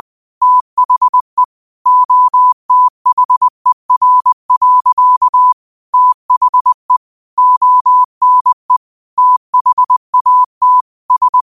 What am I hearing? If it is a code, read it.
Morse code, heard as a keyed tone, about 20 wpm, 1000 Hz. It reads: THE OTHER. THE ONE THAT S
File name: qotdfiles / 20wpm